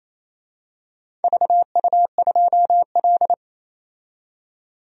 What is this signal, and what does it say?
Morse code, heard as a keyed tone, about 28 wpm, 700 Hz. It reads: VU2L
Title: DXCC-100 Call Signs - 4 Characters - 28wpm